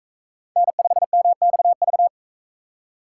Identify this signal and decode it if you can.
Morse code, heard as a keyed tone, about 44 wpm, 700 Hz. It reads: N5MXV